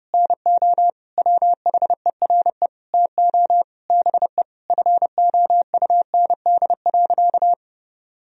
Morse code, heard as a keyed tone, about 30 wpm, 700 Hz. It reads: NO WHERE TO BE FOUND.
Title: Sentences from Top 1000 Words - Review Part 3 - 30wpm